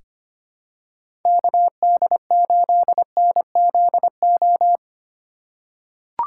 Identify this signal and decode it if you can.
Morse code, heard as a keyed tone, about 25 wpm, 700 Hz. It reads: KD8NZO
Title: USA Call Signs - 6 Characters - 4x Character Spacing - 25wpm